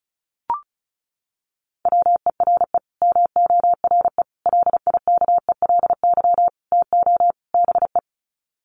Morse code, heard as a keyed tone, about 34 wpm, 700 Hz. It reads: WERE MORE LIKELY TO BE
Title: Sets of 5 Words Encore2 35wpm